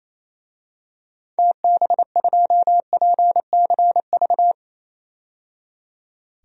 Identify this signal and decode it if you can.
Morse code, heard as a keyed tone, about 28 wpm, 700 Hz. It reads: TB2PCV
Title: DXCC-100 Call Signs - 6 Characters - 28wpm